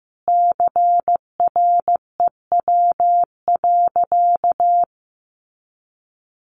Morse code, heard as a keyed tone, about 14 wpm, 700 Hz. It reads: CREW.
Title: Sentences from Top 1500 Words 15wpm